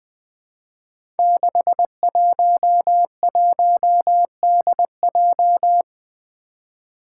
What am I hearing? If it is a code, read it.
Morse code, heard as a keyed tone, about 20 wpm, 700 Hz. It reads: 611DJ
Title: Five Challenging Letter-Number Combinations 20wpm